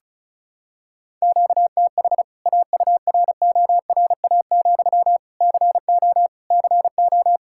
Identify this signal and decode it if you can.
Morse code, heard as a keyed tone, about 36 wpm, 700 Hz. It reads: QTH AURORA, CO CO
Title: K1USN SST QSO Practice 35wpm